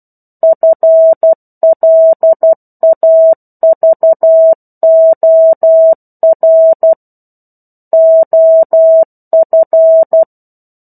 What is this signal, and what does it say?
Morse code, heard as a keyed tone, about 12 wpm, 650 Hz. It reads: FLAVOR OF
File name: 01-Chapter-01 12 WPM